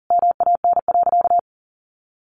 Morse code, heard as a keyed tone, about 37 wpm, 700 Hz. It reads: MAN.
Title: Sentences from Top 1300 Words 40wpm